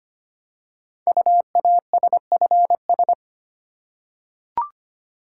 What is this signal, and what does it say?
Morse code, heard as a keyed tone, about 25 wpm, 690 Hz. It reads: UASFS
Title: Code Groups - 5 Letters - 25wpm